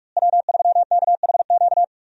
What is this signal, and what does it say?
Morse code, heard as a keyed tone, about 46 wpm, 700 Hz. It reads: W3KHQ
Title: USA Call Signs - 5 Characters - Repeated 2x - 45wpm